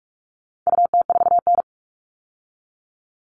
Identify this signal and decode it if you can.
Morse code, heard as a keyed tone, about 44 wpm, 700 Hz. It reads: UT4N
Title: All Cut Numbers - 4-Digit 45wpm